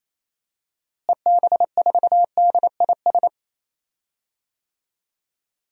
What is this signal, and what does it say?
Morse code, heard as a keyed tone, about 27 wpm, 700 Hz. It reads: EB4DIS
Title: DXCC-100 Call Signs - 6 Characters - Repeated 3x - 28wpm